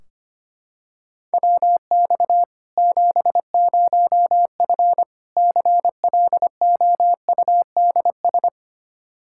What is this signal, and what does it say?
Morse code, heard as a keyed tone, about 25 wpm, 700 Hz. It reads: WX 70F CLOUDS
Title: K1USN SST QSO Practice 25wpm